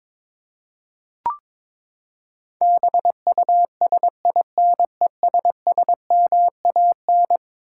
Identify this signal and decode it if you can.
Morse code, heard as a keyed tone, about 22 wpm, 700 Hz. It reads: BUSINESSMAN
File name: practice / 22wpm / Eleven Letter Words - 22wpm